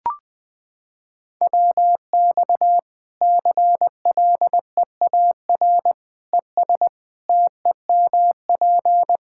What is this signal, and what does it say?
Morse code, heard as a keyed tone, about 20 wpm, 700 Hz. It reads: WX CLEAR ES TEMP
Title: Practice QSO - Exchange 2 - Info - 20wpm